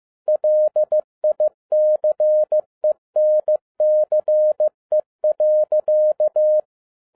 Morse code, heard as a keyed tone, about 15 wpm, 610 Hz. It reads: LICENCE.
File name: 2011-07-mp3 / n07 A LOT HAS CHANGED 15wpm